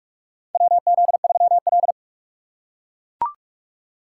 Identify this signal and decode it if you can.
Morse code, heard as a keyed tone, about 43 wpm, 700 Hz. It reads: WZ3L
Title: USA Call Signs - 4 Characters - 2x Character Spacing - 45wpm